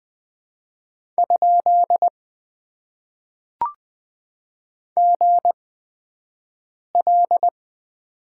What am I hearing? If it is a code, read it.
Morse code, heard as a keyed tone, about 20 wpm, spaced as for 5 wpm, 700 Hz. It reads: ? GL
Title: Top 90 QSO Elements - 8x Character Spacing - 20wpm